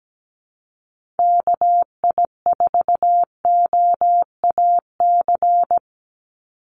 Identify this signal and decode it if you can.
Morse code, heard as a keyed tone, about 17 wpm, 700 Hz. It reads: KI4OAC
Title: USA Call Signs 17wpm